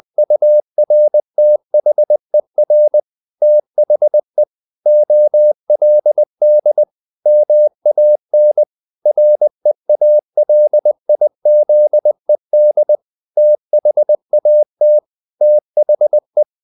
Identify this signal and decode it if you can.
Morse code, heard as a keyed tone, about 20 wpm, 600 Hz. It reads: URTHER THE OLD MAN REALIZED THAT THE
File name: the old man and the sea / the old man and the sea part 1 20wpm